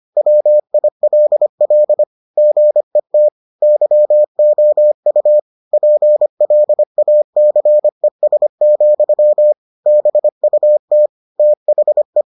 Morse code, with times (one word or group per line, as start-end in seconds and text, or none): 0.2-2.0 WILL
2.4-3.3 GET
3.6-5.4 YOU
5.7-9.5 PLACES,
9.9-11.1 BUT
11.4-12.2 THE